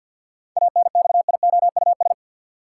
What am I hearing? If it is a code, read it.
Morse code, heard as a keyed tone, about 51 wpm, 700 Hz. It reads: ANXIOUS
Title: Seven Letter Words - 50wpm